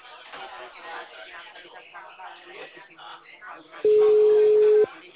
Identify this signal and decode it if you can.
Ringback tone (Europe, 425 Hz), with speech.
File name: out- -2009- - - .2